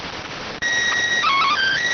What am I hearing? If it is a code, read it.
SSTV (slow-scan television), PD 290.